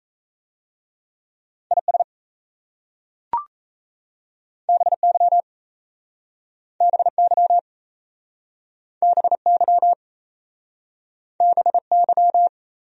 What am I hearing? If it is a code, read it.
Morse code, heard as a keyed tone, about 35 wpm, 700 Hz. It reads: IS BY BY BY BY
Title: Speed-Racing Top 200 Words 28wpm